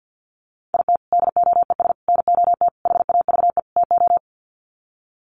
Morse code, heard as a keyed tone, about 50 wpm, 700 Hz. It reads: IT DOES NOT HAVE TO
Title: Sets of 5 Words 50wpm